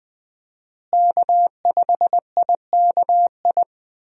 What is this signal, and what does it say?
Morse code, heard as a keyed tone, about 21 wpm, 700 Hz. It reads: K5IKI
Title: USA Call Signs - 5 Characters - Repeated 3x - 20wpm